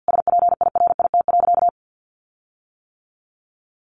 Morse code, heard as a keyed tone, about 50 wpm, 700 Hz. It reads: SPIRIT.
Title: Sentences from Top 1400 Words 50wpm